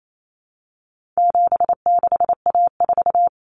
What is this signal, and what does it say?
Morse code, heard as a keyed tone, about 28 wpm, 700 Hz. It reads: 76A4
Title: Common Cut Numbers - 4-Digit 28wpm